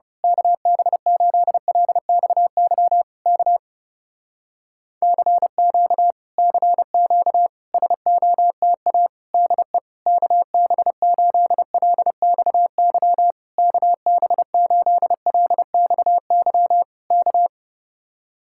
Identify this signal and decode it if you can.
Morse code, heard as a keyed tone, about 29 wpm, 700 Hz. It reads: KB8LXY K CQ CQ SOTA DE KB8LXY KB8LXY K
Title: Speed-Racing Common QSO Phrases Encore - 30wpm